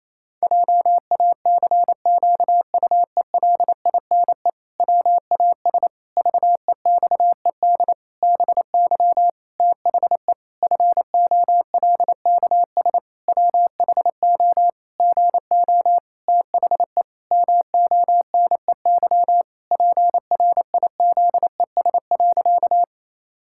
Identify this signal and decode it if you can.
Morse code, heard as a keyed tone, about 28 wpm, 700 Hz. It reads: JACQUELINE WAS VEXED BY THE FOLKS WHO GO THE MONEY PRIZES.
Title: Pangram Challenge v2 - 28wpm